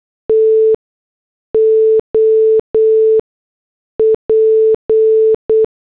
Morse code, heard as a keyed tone, about 8 wpm, spaced as for 6 wpm, 440 Hz. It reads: TOP